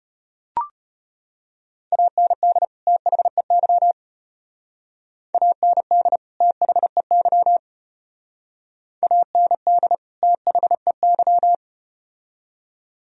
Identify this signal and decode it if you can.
Morse code, heard as a keyed tone, about 34 wpm, 700 Hz. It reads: AND THEY AND THEY AND THEY